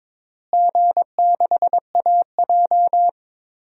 Morse code, heard as a keyed tone, about 22 wpm, 700 Hz. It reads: G6AJ
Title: DXCC-100 Call Signs - 4 Characters - Repeated 2x - 22wpm